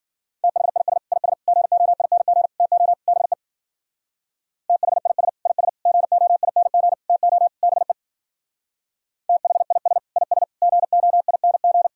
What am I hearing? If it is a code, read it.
Morse code, heard as a keyed tone, about 49 wpm, 700 Hz. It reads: THIS IS GOING TO BE THIS IS GOING TO BE THIS IS GOING
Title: Speed-Racing - Sets of 5 Words 40wpm